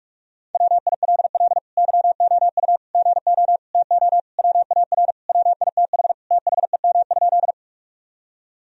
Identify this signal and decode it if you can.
Morse code, heard as a keyed tone, about 45 wpm, 700 Hz. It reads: WILL YOU GO TO WAR WITH THEM?
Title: Sentences from Top 1000 Words - Review Part 2 - 45wpm